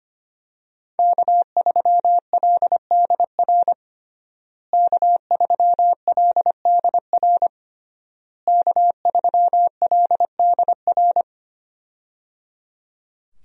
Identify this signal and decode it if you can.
Morse code, heard as a keyed tone, about 25 wpm, 700 Hz. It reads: K3LDR K3LDR K3LDR
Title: USA Call Signs - 5 Characters - Repeated 3x - 25wpm